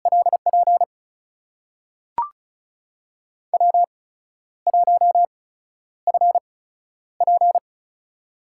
Morse code, heard as a keyed tone, about 36 wpm, spaced as for 12 wpm, 700 Hz. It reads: LP W1FP
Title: USA Call Signs - 4 Characters - 8x Character Spacing - 35wpm